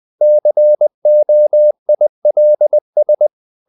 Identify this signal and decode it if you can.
Morse code, heard as a keyed tone, about 20 wpm, 600 Hz. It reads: COILS
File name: the old man and the sea part 2 20wpm